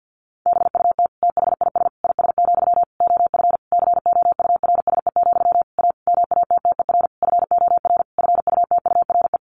Morse code, heard as a keyed tone, about 50 wpm, 700 Hz. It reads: BUT THIS IS, OF COURSE, A MATTER FOR FUTURE
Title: Sentences from Top 1500 Words 50wpm